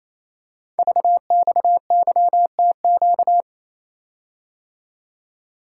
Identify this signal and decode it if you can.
Morse code, heard as a keyed tone, about 28 wpm, 700 Hz. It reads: VXYTQ
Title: Code Groups - 5 Letter:Numbers - 28wpm